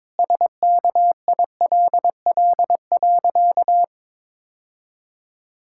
Morse code, heard as a keyed tone, about 22 wpm, 700 Hz. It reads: SKILL.
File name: practice / 22wpm / Sentences with 3-Word Phrases - 22wpm